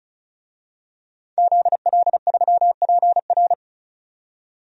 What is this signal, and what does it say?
Morse code, heard as a keyed tone, about 36 wpm, 700 Hz. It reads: ZL3PR
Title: DXCC-100 Call Signs - 5 Characters - 35wpm